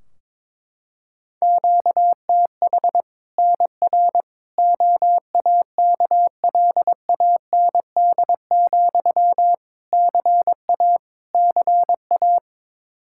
Morse code, heard as a keyed tone, about 21 wpm, 700 Hz. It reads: QTH NR OAKLAND, CA CA
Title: K1USN SST QSO Practice 22wpm